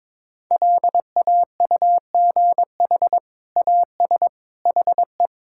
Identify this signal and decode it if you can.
Morse code, heard as a keyed tone, about 22 wpm, 700 Hz. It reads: LAUGH AS HE
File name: practice / 22wpm / Sentences with 2-Word Phrases - 22wpm